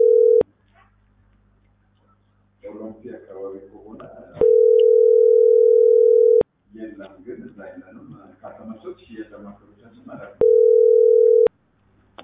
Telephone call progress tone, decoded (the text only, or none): ringback tone